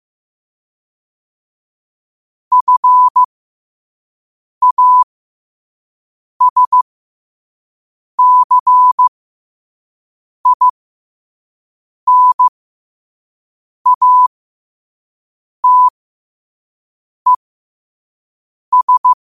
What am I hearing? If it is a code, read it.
Morse code, heard as a keyed tone, about 15 wpm, spaced as for 5 wpm, 1000 Hz. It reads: FASCINATES